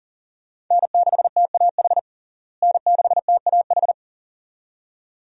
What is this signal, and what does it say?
Morse code, heard as a keyed tone, about 40 wpm, 700 Hz. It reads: N6TAH N6TAH
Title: USA Call Signs - 5 Characters - Repeated 2x - 40wpm